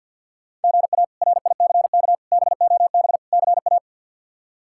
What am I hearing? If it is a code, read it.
Morse code, heard as a keyed tone, about 52 wpm, 700 Hz. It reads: GA RICK BOB CA
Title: K1USN SST QSO Practice - 50wpm